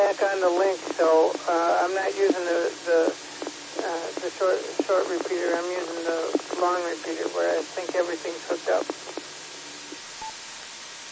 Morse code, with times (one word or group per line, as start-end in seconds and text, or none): none